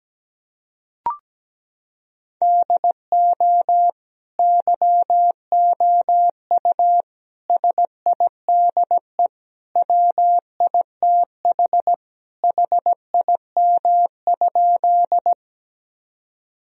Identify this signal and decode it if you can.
Morse code, heard as a keyed tone, about 17 wpm, 700 Hz. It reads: DO YOU SIDE WITH HIM?
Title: Sentences from Top 1000 Words - Review Part 1 - 17wpm